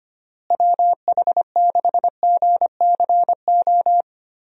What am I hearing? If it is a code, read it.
Morse code, heard as a keyed tone, about 25 wpm, 700 Hz. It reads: WH6GCO
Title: USA Call Signs - 6 Characters - Repeated 3x - 25wpm